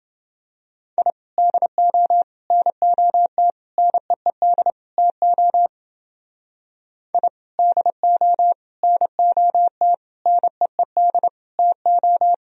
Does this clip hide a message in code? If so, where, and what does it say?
Yes: Morse code from 1.0 to 12.3 s, heard as a keyed tone, about 29 wpm, 700 Hz. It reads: I DO NOT NEED TO I DO NOT NEED TO